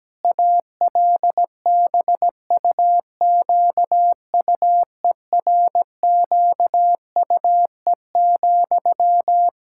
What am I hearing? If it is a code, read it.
Morse code, heard as a keyed tone, about 17 wpm, 700 Hz. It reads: ALBUQUERQUE,
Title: Practice QSO - Exchange 1 - Beginning - 17wpm